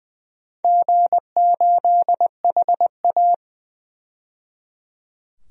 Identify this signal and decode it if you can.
Morse code, heard as a keyed tone, about 20 wpm, 700 Hz. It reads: G8HA
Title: DXCC-100 Call Signs - 4 Characters - 20wpm